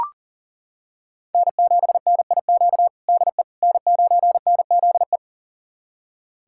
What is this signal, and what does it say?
Morse code, heard as a keyed tone, about 37 wpm, 700 Hz. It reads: N7NIQ DE N9NZE
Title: USA Callsign DE Callsign - 5 Characters - 40wpm